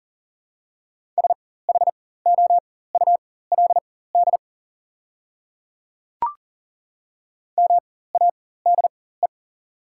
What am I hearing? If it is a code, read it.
Morse code, heard as a keyed tone, about 40 wpm, spaced as for 19 wpm, 700 Hz. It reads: SHOULD MADE
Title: 4x Character Spacing - Top 300 Words - 40wpm